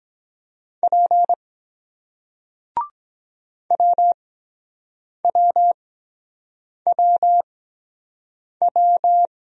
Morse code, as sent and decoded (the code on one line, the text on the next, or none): .--. / .-- .-- .-- .--
P WWWW